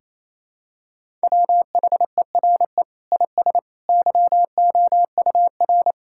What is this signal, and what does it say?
Morse code, heard as a keyed tone, about 28 wpm, 700 Hz. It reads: WHERE IS YOUR
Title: Sentences from Top 1000 Words - Review Part 5 - 28wpm